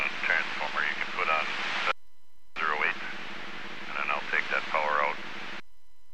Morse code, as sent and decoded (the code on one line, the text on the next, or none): none